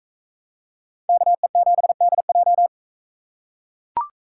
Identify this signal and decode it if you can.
Morse code, heard as a keyed tone, about 42 wpm, 700 Hz. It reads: KE7DJ